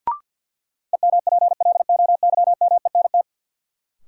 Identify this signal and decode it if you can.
Morse code, heard as a keyed tone, about 50 wpm, 700 Hz. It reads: EMPLOYMENT
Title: Sentences from Top 1500 Words - New Word Drill - 50wpm